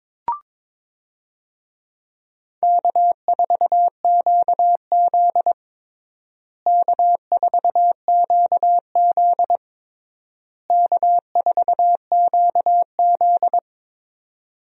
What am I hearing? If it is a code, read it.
Morse code, heard as a keyed tone, about 22 wpm, 700 Hz. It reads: K4QZ K4QZ K4QZ
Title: USA Call Signs - 4 Characters - Repeated 3x - 22wpm